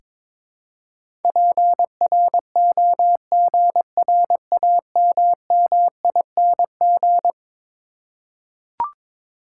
Morse code, heard as a keyed tone, about 22 wpm, 700 Hz. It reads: PROGRAMMING
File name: practice / 22wpm / Eleven Letter Words - 22wpm